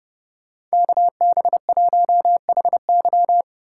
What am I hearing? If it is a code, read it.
Morse code, heard as a keyed tone, about 28 wpm, 700 Hz. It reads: KB1HY